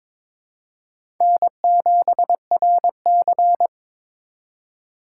Morse code, heard as a keyed tone, about 22 wpm, 700 Hz. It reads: N7RC
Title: USA Call Signs - 4 Characters - 3x Character Spacing - 22wpm